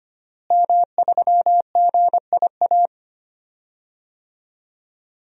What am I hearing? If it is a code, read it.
Morse code, heard as a keyed tone, about 25 wpm, 700 Hz. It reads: M3GIA
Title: DXCC-100 Call Signs - 5 Characters - Repeated 2x - 25wpm